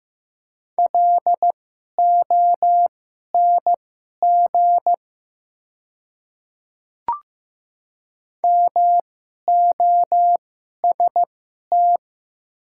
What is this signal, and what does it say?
Morse code, heard as a keyed tone, about 15 wpm, spaced as for 11 wpm, 700 Hz. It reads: LONG MOST
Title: 2x Character Spacing - Top 100 Words - 15wpm